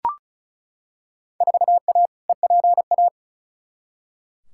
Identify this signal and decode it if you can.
Morse code, heard as a keyed tone, about 34 wpm, 700 Hz. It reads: 4A EPA
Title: Field Day Exchanges - 35wpm